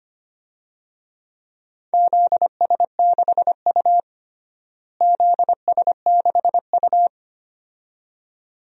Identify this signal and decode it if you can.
Morse code, heard as a keyed tone, about 25 wpm, 700 Hz. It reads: ZS6U ZS6U
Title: DXCC-100 Call Signs - 4 Characters - Repeated 2x - 25wpm